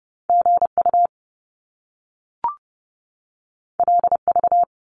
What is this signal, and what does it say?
Morse code, heard as a keyed tone, about 30 wpm, 700 Hz. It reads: GU LV
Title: Top Two Letter Combinations in Words 30wpm